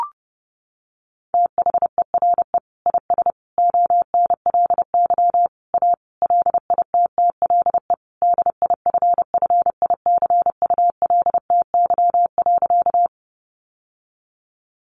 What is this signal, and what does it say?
Morse code, heard as a keyed tone, about 30 wpm, 700 Hz. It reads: THERE IS ONLY A LITTLE DIFFICULTY.
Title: Sentences from Top 1500 Words 30wpm